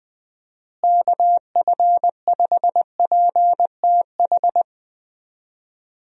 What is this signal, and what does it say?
Morse code, heard as a keyed tone, about 20 wpm, 700 Hz. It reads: KF5PTH
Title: USA Call Signs - 6 Characters - Repeated 2x - 20wpm